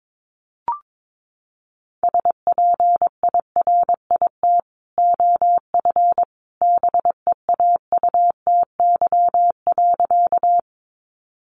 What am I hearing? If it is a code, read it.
Morse code, heard as a keyed tone, about 22 wpm, 700 Hz. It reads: SPIRIT OF BEAUTY.
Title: Sentences from Top 1400 Words 22wpm